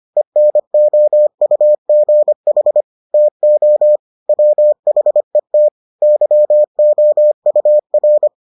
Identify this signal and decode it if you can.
Morse code, heard as a keyed tone, about 25 wpm, 600 Hz. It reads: ENOUGH TO WHET YOUR